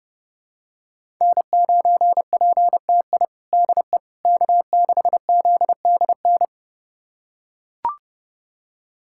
Morse code, heard as a keyed tone, about 30 wpm, 700 Hz. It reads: N9PTI DE K6ZDN
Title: USA Callsign DE Callsign - 5 Characters - 30wpm